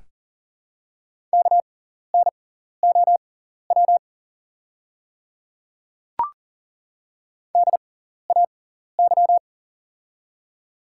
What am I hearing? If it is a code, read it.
Morse code, heard as a keyed tone, about 41 wpm, spaced as for 14 wpm, 700 Hz. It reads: KNOW DAY
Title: 6x Character Spacing - Top 100 Words - 40wpm